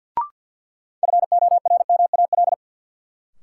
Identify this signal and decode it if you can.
Morse code, heard as a keyed tone, about 52 wpm, 700 Hz. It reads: FORMAL
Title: Sentences from Top 1300 Words - New Word Drill - 50wpm